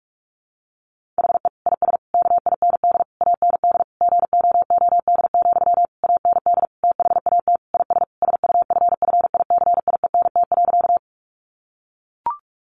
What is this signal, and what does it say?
Morse code, heard as a keyed tone, about 44 wpm, 700 Hz. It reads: HE IS KIND AND GOOD, AND THAT IS SUFFICIENT.
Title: Sentences from Top 1400 Words 45wpm